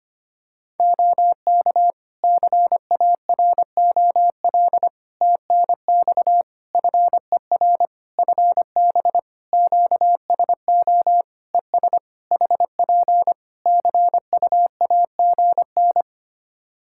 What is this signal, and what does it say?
Morse code, heard as a keyed tone, about 25 wpm, 700 Hz. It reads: OK CAROL TNX FER FB QSO ES HP CUAGN
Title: Practice QSO - Exchange 4 - Ending - 25wpm